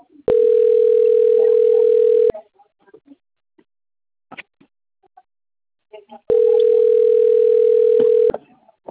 Ringback tone (North America), 440 + 480 Hz.